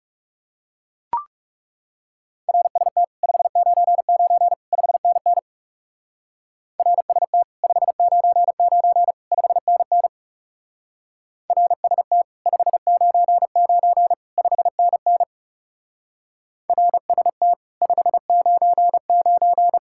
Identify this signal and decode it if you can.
Morse code, heard as a keyed tone, about 39 wpm, 700 Hz. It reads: RST 599 5NN RST 599 5NN RST 599 5NN RST 599